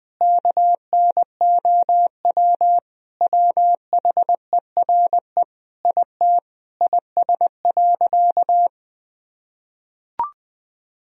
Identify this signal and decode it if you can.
Morse code, heard as a keyed tone, about 20 wpm, 700 Hz. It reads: KNOW WHERE IT IS.
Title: Sentences from Top 1000 Words - Review Part 4 - 20wpm